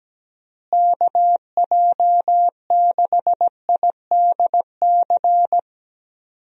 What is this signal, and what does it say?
Morse code, heard as a keyed tone, about 17 wpm, 700 Hz. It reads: KJ6IDC
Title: USA Call Signs - 6 Characters - Repeated 2x - 17wpm